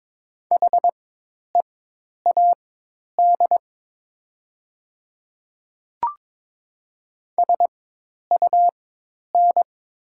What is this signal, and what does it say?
Morse code, heard as a keyed tone, about 22 wpm, spaced as for 10 wpm, 700 Hz. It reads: HEAD SUN
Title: 4x Character Spacing - Top 500 Words - 22wpm